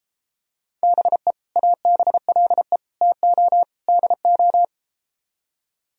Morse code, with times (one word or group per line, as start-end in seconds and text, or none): 0.8-1.3 BE
1.6-2.8 ABLE
3.0-3.6 TO
3.9-4.6 DO